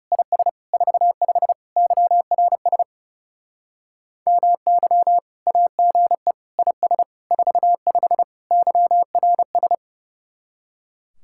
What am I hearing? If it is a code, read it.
Morse code, heard as a keyed tone, about 38 wpm, 700 Hz. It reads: IS 45 YRS MY AGE IS 45 YRS